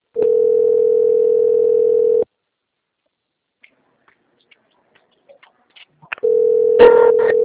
Ringback tone (North America), 440 + 480 Hz.